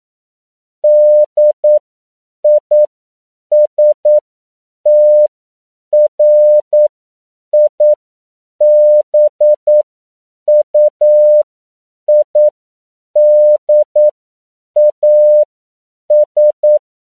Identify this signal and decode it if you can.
Morse code, heard as a keyed tone, about 9 wpm, spaced as for 7 wpm, 600 Hz. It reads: DISTRIBUIDAS
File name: CW / 10PPM